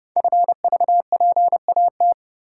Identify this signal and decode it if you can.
Morse code, heard as a keyed tone, about 30 wpm, 700 Hz. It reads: FVPAT
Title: Code Groups - 5 Letters - 30wpm